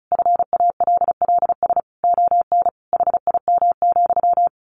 Morse code, heard as a keyed tone, about 34 wpm, 700 Hz. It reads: FALLS ON HIM,
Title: A Sound Of Thunder 35wpm